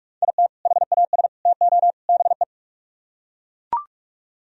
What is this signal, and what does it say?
Morse code, heard as a keyed tone, about 41 wpm, 700 Hz. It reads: IT HAS TO BE